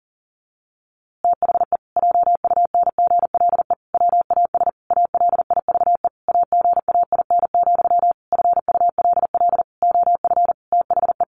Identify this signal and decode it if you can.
Morse code, heard as a keyed tone, about 40 wpm, 700 Hz. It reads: THE JUNGLE WAS ALIVE AGAIN, FULL OF THE
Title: A Sound Of Thunder 40wpm